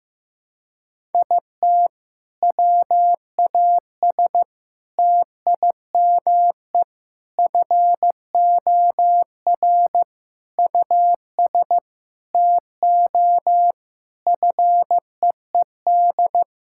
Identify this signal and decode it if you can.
Morse code, heard as a keyed tone, about 15 wpm, 700 Hz. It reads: IT WAS TIME FOR US TO FEED
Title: Sentences from Top 1600 Words 15wpm